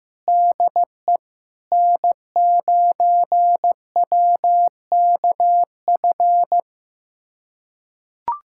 Morse code, heard as a keyed tone, about 16 wpm, 700 Hz. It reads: DE N9WKF
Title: USA Callsign DE Callsign - 5 Characters - 15wpm